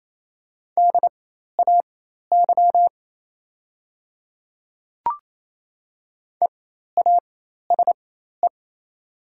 Morse code, heard as a keyed tone, about 28 wpm, spaced as for 13 wpm, 700 Hz. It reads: DAY EASE
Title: 4x Character Spacing - Top 500 Words - 28wpm